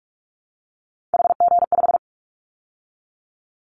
Morse code, heard as a keyed tone, about 44 wpm, 700 Hz. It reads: HG5
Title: Callsign Prefixes for Top 50 DXCC - 45wpm